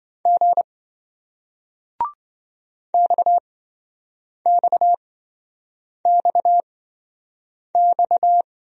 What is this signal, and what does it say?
Morse code, heard as a keyed tone, about 26 wpm, spaced as for 7 wpm, 700 Hz. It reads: G XXXX